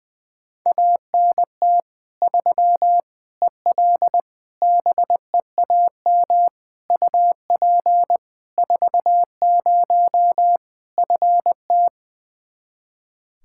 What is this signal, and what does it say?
Morse code, heard as a keyed tone, about 21 wpm, 700 Hz. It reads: ANT 3 EL BEAM UP 40 FT
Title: K1USN SST QSO Practice 20wpm